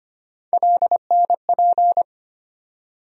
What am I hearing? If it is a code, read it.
Morse code, heard as a keyed tone, about 25 wpm, 700 Hz. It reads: LNP
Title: USA Call Sign Postfixes - 3 Letters 25wpm